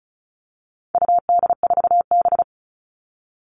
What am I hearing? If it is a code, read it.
Morse code, heard as a keyed tone, about 34 wpm, 700 Hz. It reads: UD4B
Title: All Cut Numbers - 4-Digit 35wpm